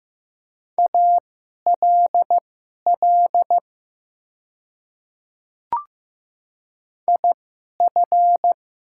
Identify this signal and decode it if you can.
Morse code, heard as a keyed tone, about 15 wpm, spaced as for 11 wpm, 700 Hz. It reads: ALL IF
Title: 2x Character Spacing - Top 100 Words - 15wpm